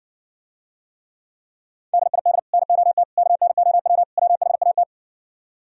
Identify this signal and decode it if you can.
Morse code, heard as a keyed tone, about 59 wpm, 700 Hz. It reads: DID NOT KNOW WHAT